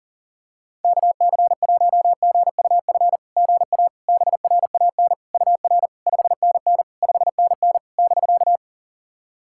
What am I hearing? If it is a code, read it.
Morse code, heard as a keyed tone, about 40 wpm, 690 Hz. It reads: KC1GUF GA BRAN UR 5NN 5NN <BK>
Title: SOTA QSO Practice 40wpm